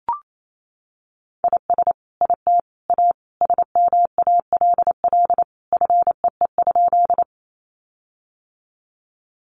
Morse code, heard as a keyed tone, about 28 wpm, 700 Hz. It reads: IS IT A SMALL FEE?
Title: Sentences from Top 1500 Words 28wpm